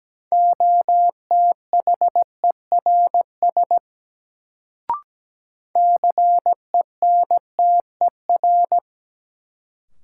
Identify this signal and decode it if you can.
Morse code, heard as a keyed tone, about 17 wpm, 700 Hz. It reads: OTHERS CENTER
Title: Sentences from Top 1000 Words - Review Part 2 - New Word Drill - 17wpm